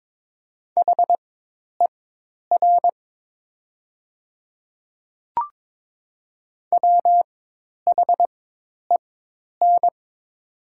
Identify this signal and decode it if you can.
Morse code, heard as a keyed tone, about 21 wpm, spaced as for 10 wpm, 690 Hz. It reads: HER WHEN
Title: 4x Character Spacing - Top 200 Words - 22wpm